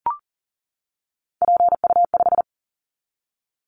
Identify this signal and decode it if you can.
Morse code, heard as a keyed tone, about 40 wpm, 700 Hz. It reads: PU5
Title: Callsign Prefixes for Top 100 DXCC - 40wpm